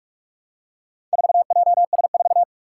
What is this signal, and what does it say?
Morse code, heard as a keyed tone, about 46 wpm, 700 Hz. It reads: 4JS4
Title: Four Challenging Letter-Number Combinations 45wpm